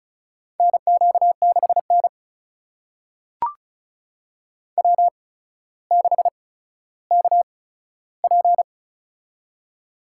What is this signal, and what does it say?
Morse code, heard as a keyed tone, about 36 wpm, spaced as for 15 wpm, 700 Hz. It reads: NQ6N W6KP
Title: USA Call Signs - 4 Characters - 8x Character Spacing - 35wpm